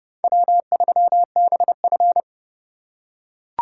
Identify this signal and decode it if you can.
Morse code, heard as a keyed tone, about 28 wpm, 700 Hz. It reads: W3BF